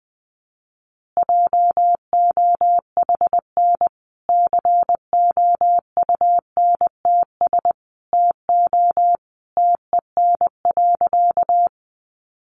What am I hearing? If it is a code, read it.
Morse code, heard as a keyed tone, about 20 wpm, 700 Hz. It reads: JOHN COUNTS TO TEN.
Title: Easy Phrases 20wpm